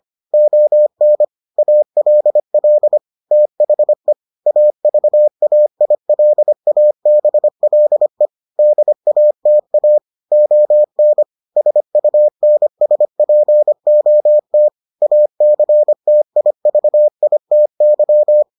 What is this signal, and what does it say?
Morse code, heard as a keyed tone, about 25 wpm, 600 Hz. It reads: ON ALL THE AVAILABLE DATA ON SUNSPOT ACTIVITY